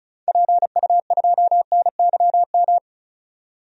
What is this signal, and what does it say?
Morse code, heard as a keyed tone, about 36 wpm, 700 Hz. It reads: PU2NYM
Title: DXCC-100 Call Signs - 6 Characters - 35wpm